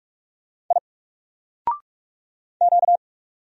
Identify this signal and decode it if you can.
Morse code, heard as a keyed tone, about 47 wpm, 700 Hz. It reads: I Q
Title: Single Character (Letter:Number:Punctuation:Prosign) 45wpm